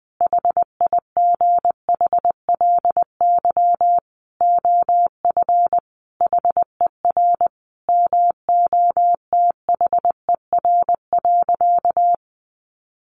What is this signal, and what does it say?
Morse code, heard as a keyed tone, about 20 wpm, 700 Hz. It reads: HIGHLY OF HER MOTHER.
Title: Sentences from Top 1100 Words 20wpm